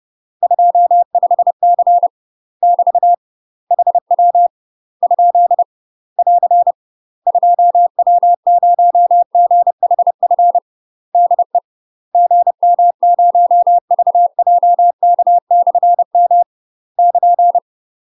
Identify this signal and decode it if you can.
Morse code, heard as a keyed tone, about 30 wpm, 700 Hz. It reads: 25C <BT> HW ? <AR> 2W0GHF DE GM0VJK/M <KN>
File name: UK MASTER QSOs TRAINING / 30WPM / UK MASTER QSOs 0039